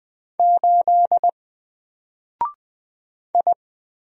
Morse code, heard as a keyed tone, about 20 wpm, 700 Hz. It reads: 8 I
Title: Single Letter-Number Alt 20wpm